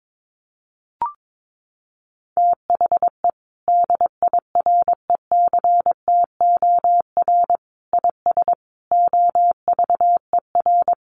Morse code, heard as a keyed tone, about 22 wpm, 700 Hz. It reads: THE DIRECTOR IS OVER
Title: Sentences from Top 1300 Words 22wpm